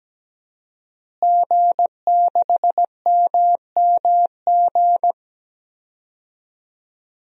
Morse code, heard as a keyed tone, about 17 wpm, 700 Hz. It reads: G6MMG
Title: DXCC-100 Call Signs - 5 Characters - Repeated 3x - 17wpm